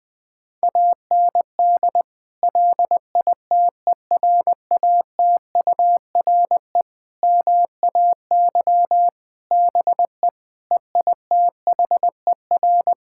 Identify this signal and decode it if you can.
Morse code, heard as a keyed tone, about 20 wpm, 700 Hz. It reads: AND LITERATURE MAY BE EITHER
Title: Sentences from Top 1600 Words 20wpm